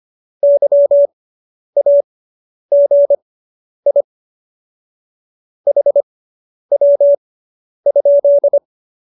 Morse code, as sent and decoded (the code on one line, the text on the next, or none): -.-- .- --. .. / .... .-- ..--..
YAGI HW?